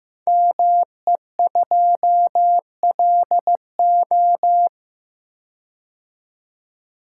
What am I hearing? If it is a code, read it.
Morse code, heard as a keyed tone, about 15 wpm, 700 Hz. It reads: ME2LO
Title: Code Groups - 5 Letter:Numbers - 15wpm